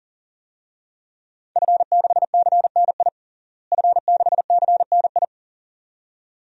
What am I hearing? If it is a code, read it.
Morse code, heard as a keyed tone, about 40 wpm, 700 Hz. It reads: F6CNI F6CNI
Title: DXCC-100 Call Signs - 5 Characters - Repeated 2x - 40wpm